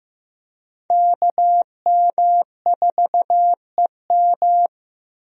Morse code, heard as a keyed tone, about 15 wpm, 700 Hz. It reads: KM4EM
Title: USA Call Signs - 5 Characters - 15wpm